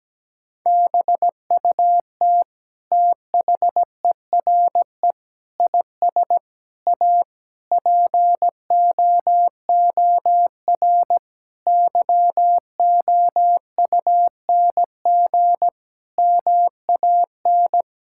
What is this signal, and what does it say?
Morse code, heard as a keyed tone, about 17 wpm, 700 Hz. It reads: BUT THERE IS A POOR YOUNG MAN
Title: Sentences from Top 1000 Words - Review Part 6 - 17wpm